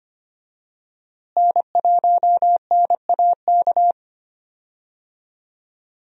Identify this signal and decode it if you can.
Morse code, heard as a keyed tone, about 25 wpm, 700 Hz. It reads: N1NAK
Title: USA Call Signs - 5 Characters - Repeated 2x - 25wpm